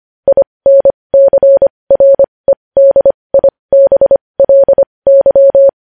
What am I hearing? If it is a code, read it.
Morse code, heard as a keyed tone, about 25 wpm, spaced as for 20 wpm, 550 Hz. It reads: INCREDIBLY
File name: podcast / News headlines-1x-25x20